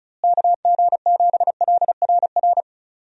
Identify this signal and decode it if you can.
Morse code, heard as a keyed tone, about 34 wpm, 700 Hz. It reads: KG7LRR
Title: USA Call Signs - 6 Characters - Repeated 3x - 35wpm